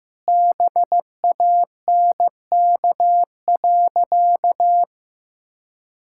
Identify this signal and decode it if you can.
Morse code, heard as a keyed tone, about 15 wpm, 700 Hz. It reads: BANK.
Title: Sentences from Top 1000 Words - Review Part 6 - 15wpm